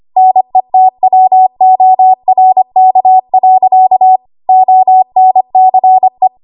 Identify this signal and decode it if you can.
Morse code, heard as a keyed tone, about 25 wpm, 750 Hz. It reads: NETWORK. ONCE
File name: Archive / 25 WPM / 25WPM